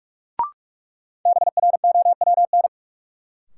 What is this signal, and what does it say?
Morse code, heard as a keyed tone, about 46 wpm, 700 Hz. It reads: BROWN
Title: Sentences from Top 1200 Words - New Word Drill Alt - 45wpm